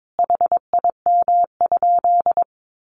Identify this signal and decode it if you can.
Morse code, heard as a keyed tone, about 22 wpm, 700 Hz. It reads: HIM?
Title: Sentences from Top 100 Words 22wpm